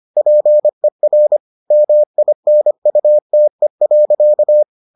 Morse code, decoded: PER MINUTE.

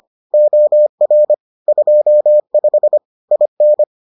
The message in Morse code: OR 25 IN

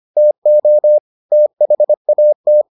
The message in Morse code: TO THAT

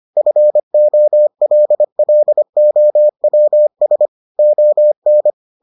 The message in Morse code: FOLLOWS ON

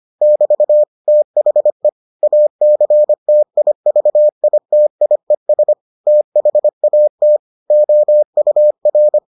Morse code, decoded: <BT> THE ACTIVITIES THAT OUR

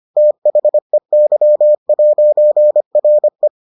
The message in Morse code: THEY'RE

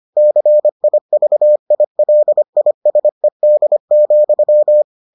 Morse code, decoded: CIVILISED,

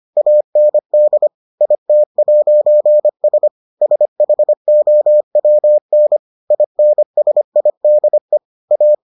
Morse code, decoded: AND IT'S SHOWN INSIDE A

